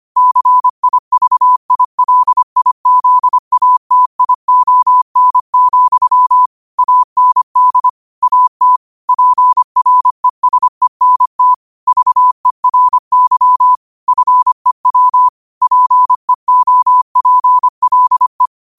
Morse code, heard as a keyed tone, about 25 wpm, 1000 Hz. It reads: CIVILIZATION, AND AT PRESENT VERY FEW PEOPLE
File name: qotdfiles / 25wpm